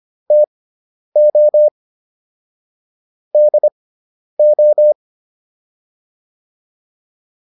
Morse code, heard as a keyed tone, about 25 wpm, spaced as for 10 wpm, 600 Hz. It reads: TO DO